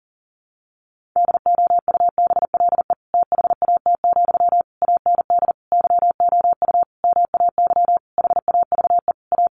Morse code, heard as a keyed tone, about 40 wpm, 700 Hz. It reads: DOUBLE THAT, AND YOU MAY HAVE A
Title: Sentences from Top 1400 Words 40wpm